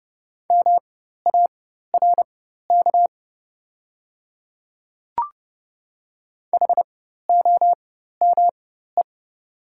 Morse code, heard as a keyed tone, about 30 wpm, spaced as for 14 wpm, 700 Hz. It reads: MARK HOME
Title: 4x Character Spacing - Top 300 Words - 30wpm